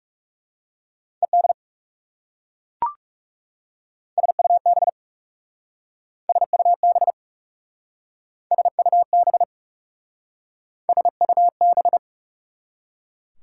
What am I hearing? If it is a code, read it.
Morse code, heard as a keyed tone, about 40 wpm, 700 Hz. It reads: ED SUB SUB SUB SUB